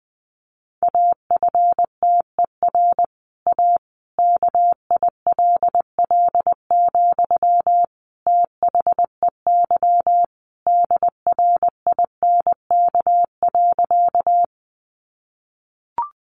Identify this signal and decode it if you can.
Morse code, heard as a keyed tone, about 19 wpm, 700 Hz. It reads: AFTER A KILL, THEY DRINK.
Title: Sentences from Top 1500 Words 20wpm